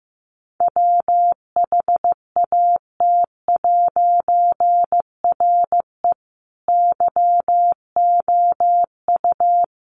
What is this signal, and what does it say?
Morse code, heard as a keyed tone, about 15 wpm, 700 Hz. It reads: WHAT'RE YOU
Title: A Sound Of Thunder 15wpm